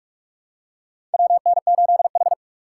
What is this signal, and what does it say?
Morse code, heard as a keyed tone, about 46 wpm, 700 Hz. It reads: WN8H
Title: USA Call Signs - 4 Characters - 8x Character Spacing - 45wpm